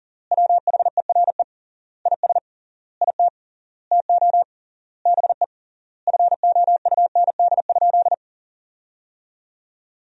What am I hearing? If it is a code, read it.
Morse code, heard as a keyed tone, about 37 wpm, 700 Hz. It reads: WHERE IS IT TO BE FOUND?
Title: 3x Word-Spacing - Sentences from Top 300 Words - 40wpm